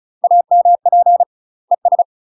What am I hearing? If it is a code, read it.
Morse code, heard as a keyed tone, about 36 wpm, 700 Hz. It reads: AMP ES
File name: UK MASTER QSOs TRAINING / 35WPM / UK MASTER QSOs 0017